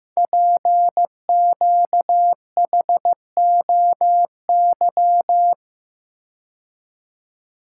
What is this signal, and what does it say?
Morse code, heard as a keyed tone, about 15 wpm, 700 Hz. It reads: PQHOY
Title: Code Groups - 5 Letters - 15wpm